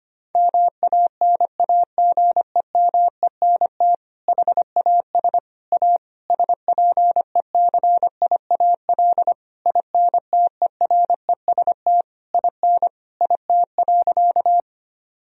Morse code, heard as a keyed tone, about 25 wpm, 700 Hz. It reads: MANAGEMENT HAS A SPECIAL INTEREST IN IT.
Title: Sentences from Top 1000 Words - Review Part 4 - 25wpm